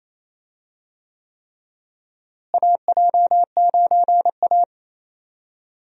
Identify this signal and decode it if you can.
Morse code, heard as a keyed tone, about 28 wpm, 700 Hz. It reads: AJ9A
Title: USA Call Signs - 4 Characters - Repeated 3x - 28wpm